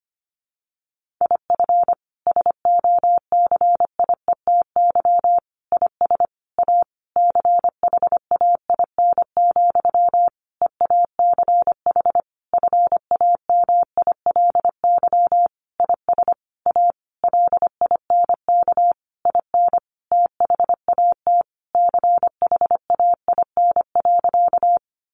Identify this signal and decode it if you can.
Morse code, heard as a keyed tone, about 25 wpm, 700 Hz. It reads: IF SOCIETY IS A CHAIN, EACH FAMILY IS A LINK IN THAT CHAIN.
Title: Sentences from Top 1500 Words 25wpm